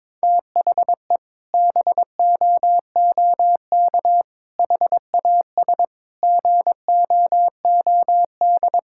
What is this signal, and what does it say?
Morse code, heard as a keyed tone, about 21 wpm, 700 Hz. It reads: THE BOOK HAS GOOD
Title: Sentences from Top 1000 Words - Review Part 6 - 22wpm